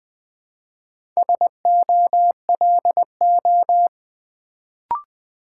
Morse code, heard as a keyed tone, about 20 wpm, 700 Hz. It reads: SOLO
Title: Build A Word - 2 Letters At A Time - 4-Letter Words - 20wpm